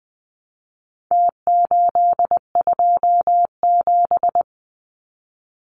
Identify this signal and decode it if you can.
Morse code, heard as a keyed tone, about 20 wpm, 700 Hz. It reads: T827
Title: Numbers - Contest Style - 4-Digit 20wpm